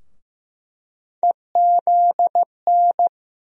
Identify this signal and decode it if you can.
Morse code, heard as a keyed tone, about 15 wpm, 700 Hz. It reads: EZN
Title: USA Call Sign Postfixes - 3 Letters 15wpm